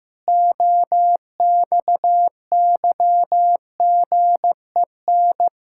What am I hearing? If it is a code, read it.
Morse code, heard as a keyed tone, about 15 wpm, 700 Hz. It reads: OXYGEN
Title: Pangram Challenge v2 - 15wpm